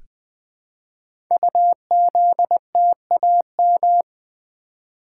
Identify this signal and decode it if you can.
Morse code, heard as a keyed tone, about 20 wpm, 700 Hz. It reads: UZTAM
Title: Code Groups - 5 Letters - 20wpm